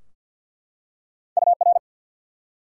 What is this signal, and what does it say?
Morse code, heard as a keyed tone, about 48 wpm, 700 Hz. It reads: UR
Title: Top 90 QSO Elements 50wpm